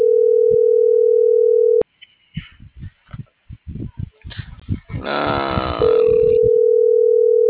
Ringback tone (North America), 440 + 480 Hz.